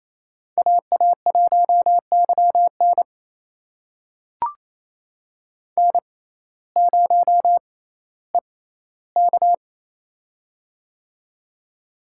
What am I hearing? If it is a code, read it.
Morse code, heard as a keyed tone, about 28 wpm, spaced as for 15 wpm, 700 Hz. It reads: AA1YN N0EK
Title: USA Call Signs - 6x Character Spacing - 28wpm